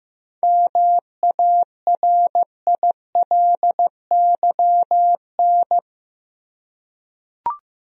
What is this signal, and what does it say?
Morse code, heard as a keyed tone, about 15 wpm, 700 Hz. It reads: MARILYN
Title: USA Common Names - 15wpm